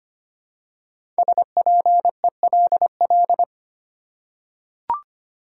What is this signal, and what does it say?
Morse code, heard as a keyed tone, about 25 wpm, 700 Hz. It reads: SPELL
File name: 5 Letter Words - First 3 Letters Given - 25wpm